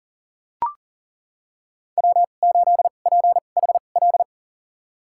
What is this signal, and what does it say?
Morse code, heard as a keyed tone, about 37 wpm, spaced as for 28 wpm, 700 Hz. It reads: W8PHL
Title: USA Call Signs - 5 Characters - 2x Character Spacing - 40wpm